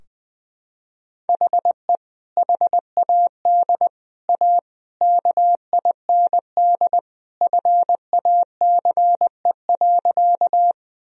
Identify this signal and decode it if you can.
Morse code, heard as a keyed tone, about 20 wpm, 700 Hz. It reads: HE HAD A KIND FACE.
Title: Sentences from Top 1000 Words - Review Part 2 - 20wpm